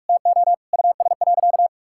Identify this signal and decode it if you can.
Morse code, heard as a keyed tone, about 46 wpm, 700 Hz. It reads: TO US.
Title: Sentences from Top 1600 Words 45wpm